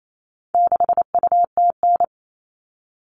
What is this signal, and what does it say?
Morse code, heard as a keyed tone, about 28 wpm, 700 Hz. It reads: 6UTN
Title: All Cut Numbers - 4-Digit 28wpm